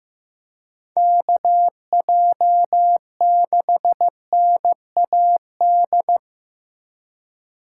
Morse code, heard as a keyed tone, about 15 wpm, 700 Hz. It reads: KJ6NAD
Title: USA Call Signs - 6 Characters - 3x Character Spacing - 15wpm